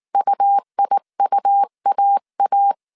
Morse code, heard as a keyed tone, about 19 wpm, 800 Hz. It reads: UIUAA
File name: CW / DET / 1 3 1500C S B 18wpm 1916